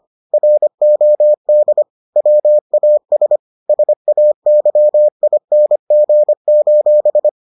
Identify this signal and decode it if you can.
Morse code, heard as a keyed tone, about 25 wpm, 600 Hz. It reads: ROD WAS SAYING: